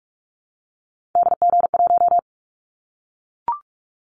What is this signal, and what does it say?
Morse code, heard as a keyed tone, about 44 wpm, 700 Hz. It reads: DG1
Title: Callsign Prefixes for Top 25 DXCC - 45wpm